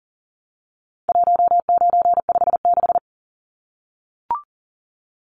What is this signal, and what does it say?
Morse code, heard as a keyed tone, about 40 wpm, 700 Hz. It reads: 1956
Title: Numbers - 4-Digit 40wpm